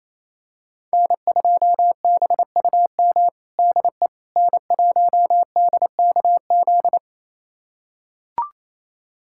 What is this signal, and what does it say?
Morse code, heard as a keyed tone, about 28 wpm, 700 Hz. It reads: N2BUM DE N1DKZ
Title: USA Callsign DE Callsign - 5 Characters - 28wpm